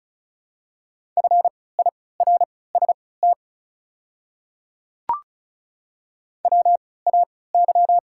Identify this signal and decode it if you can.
Morse code, heard as a keyed tone, about 34 wpm, spaced as for 20 wpm, 700 Hz. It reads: FIRST WAY
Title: 3x Character Spacing - Top 300 Words - 35wpm